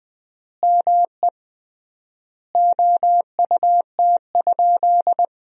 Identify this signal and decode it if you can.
Morse code, heard as a keyed tone, about 20 wpm, 700 Hz. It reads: ME OUT?
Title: 3x Word-Spacing - Sentences from Top 200 Words - 20wpm